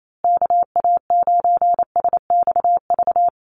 Morse code, heard as a keyed tone, about 28 wpm, 700 Hz. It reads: KA9SXV
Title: Call Signs Repeated 3x - 28wpm